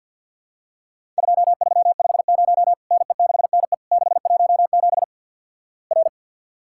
Morse code, heard as a keyed tone, about 50 wpm, 700 Hz. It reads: 2350 NE6NE 617 R
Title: Worked All Europe DX Contest - QTC Exchange - 50wpm